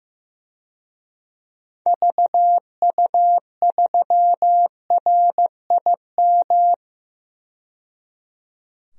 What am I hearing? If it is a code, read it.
Morse code, heard as a keyed tone, about 15 wpm, 700 Hz. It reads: VU3RIM
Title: DXCC-100 Call Signs - 6 Characters - 15wpm